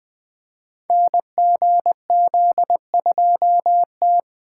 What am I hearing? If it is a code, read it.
Morse code, heard as a keyed tone, about 20 wpm, 700 Hz. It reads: NGZ2T
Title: Code Groups - 5 Letter:Numbers - 20wpm